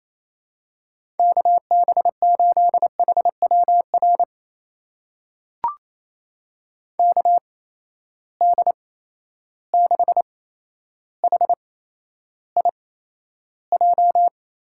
Morse code, heard as a keyed tone, about 28 wpm, spaced as for 12 wpm, 700 Hz. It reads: KB8HWR KD6HIJ
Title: USA Call Signs - 6 Characters - 8x Character Spacing - 28wpm